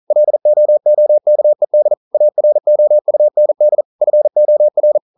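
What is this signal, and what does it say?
Morse code, heard as a keyed tone, about 42 wpm, 600 Hz. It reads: LOOKED AROUND FOR
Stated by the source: Morse code practice at 41 wpm, from CW med Gnister.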